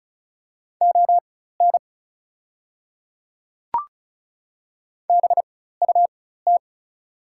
Morse code, heard as a keyed tone, about 36 wpm, spaced as for 16 wpm, 700 Hz. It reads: ON BUT
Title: 4x Character Spacing - Top 100 Words - 35wpm